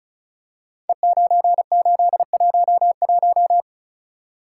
Morse code, heard as a keyed tone, about 36 wpm, 700 Hz. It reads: E9811
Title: Code Groups - 5 Letter:Numbers - 35wpm